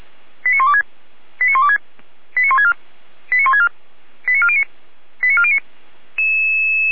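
Five-tone selective call (ZVEI), heard as quick runs of five tones, tones of about 70 ms. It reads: 89126 89126 89164 89164 89409 89409
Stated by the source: ZVEI five-tone selective call received via SDR.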